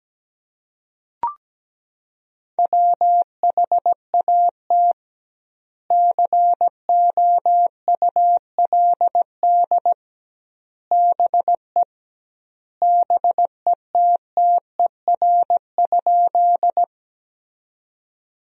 Morse code, heard as a keyed tone, about 17 wpm, 700 Hz. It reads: WHAT COULD BE BETTER?
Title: 2x Word-Spacing - Sentences from Top 400 Words - 17wpm